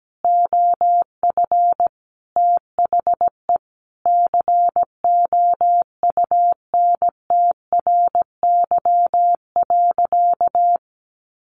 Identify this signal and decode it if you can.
Morse code, heard as a keyed tone, about 17 wpm, 700 Hz. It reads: OF THE COUNTRY.
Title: Sentences from Top 300 Words 17wpm